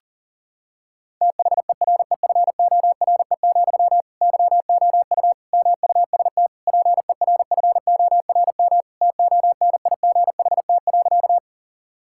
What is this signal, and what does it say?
Morse code, heard as a keyed tone, about 40 wpm, 690 Hz. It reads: THEREFORE, YOU MUST PERFORM TONIGHT.
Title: Sentences from Top 1000 Words - Review Part 5 - 40wpm